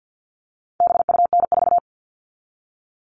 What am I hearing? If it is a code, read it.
Morse code, heard as a keyed tone, about 50 wpm, 700 Hz. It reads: BUN4
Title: All Cut Numbers - 4-Digit 50wpm